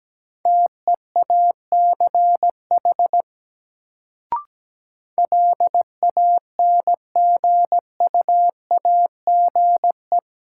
Morse code, heard as a keyed tone, about 17 wpm, 700 Hz. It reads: TEACH LANGUAGE
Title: Sentences from Top 500 Words - New Word Drill - 17wpm